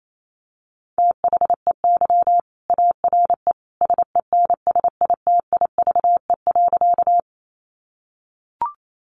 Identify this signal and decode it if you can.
Morse code, heard as a keyed tone, about 28 wpm, 700 Hz. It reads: THEY ARE SENSITIVE.
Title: Sentences from Top 1400 Words 28wpm